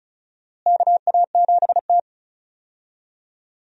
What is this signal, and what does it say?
Morse code, heard as a keyed tone, about 34 wpm, 700 Hz. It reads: KA7T
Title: USA Call Signs - 4 Characters - Repeated 2x - 35wpm